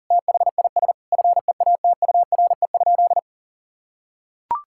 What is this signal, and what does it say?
Morse code, heard as a keyed tone, about 40 wpm, 700 Hz. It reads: THIS FEATURE?
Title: Sentences from Top 1000 Words - Review Part 5 - 40wpm